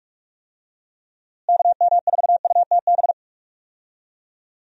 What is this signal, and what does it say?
Morse code, heard as a keyed tone, about 46 wpm, 700 Hz. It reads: KM4UTB
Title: USA Call Signs - 6 Characters - Repeated 2x - 45wpm